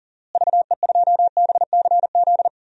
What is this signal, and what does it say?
Morse code, heard as a keyed tone, about 40 wpm, 700 Hz. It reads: VE2BCZ
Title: DXCC-100 Call Signs - 6 Characters - Repeated 2x - 40wpm